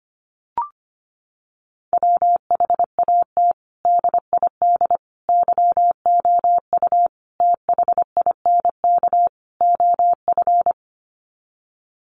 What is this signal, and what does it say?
Morse code, heard as a keyed tone, about 25 wpm, 700 Hz. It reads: WHAT DID YOU THINK OF
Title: Sets of 5 Words Encore2 25wpm